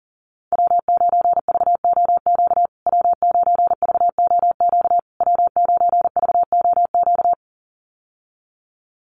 Morse code, heard as a keyed tone, about 38 wpm, 700 Hz. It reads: W9VOQ W9VOQ W9VOQ
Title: USA Call Signs Repeated 3x 40wpm